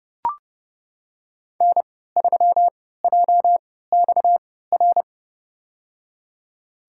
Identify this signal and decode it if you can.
Morse code, heard as a keyed tone, about 30 wpm, spaced as for 17 wpm, 700 Hz. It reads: N3JXR
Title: USA Call Signs - 3x Character Spacing - 30wpm